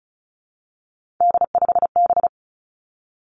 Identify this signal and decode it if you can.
Morse code, heard as a keyed tone, about 36 wpm, 700 Hz. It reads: D5B
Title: Three Challenging Letter-Number Combinations 35wpm